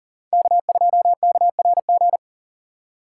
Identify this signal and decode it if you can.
Morse code, heard as a keyed tone, about 38 wpm, 700 Hz. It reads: K2KRG
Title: USA Call Signs - 5 Characters - 3x Character Spacing - 40wpm